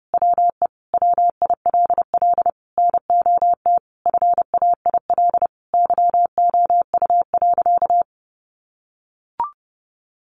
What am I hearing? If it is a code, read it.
Morse code, heard as a keyed tone, about 30 wpm, 700 Hz. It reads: WE WILL NOT FAIL YOU.
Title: Sentences from Top 1100 Words 30wpm